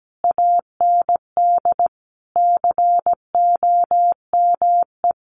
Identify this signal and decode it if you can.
Morse code, heard as a keyed tone, about 17 wpm, 700 Hz. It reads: AND COME
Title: Sentences from Top 1000 Words 17wpm